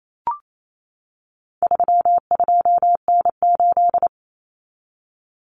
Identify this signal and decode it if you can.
Morse code, heard as a keyed tone, about 28 wpm, 700 Hz. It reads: 32N8
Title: Common Cut Numbers - 4-Digit 28wpm